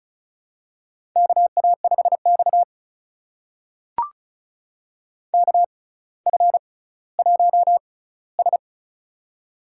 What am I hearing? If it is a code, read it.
Morse code, heard as a keyed tone, about 36 wpm, spaced as for 18 wpm, 700 Hz. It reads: KA5X KF1S
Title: USA Call Signs - 4 Characters - 6x Character Spacing - 35wpm